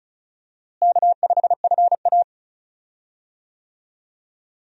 Morse code, heard as a keyed tone, about 34 wpm, 700 Hz. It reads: K5FA
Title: USA Call Signs - 4 Characters - Repeated 2x - 35wpm